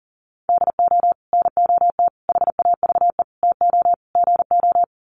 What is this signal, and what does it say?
Morse code, heard as a keyed tone, about 40 wpm, 700 Hz. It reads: DO NOT HAVE TO GO
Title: Sets of 5 Words 40wpm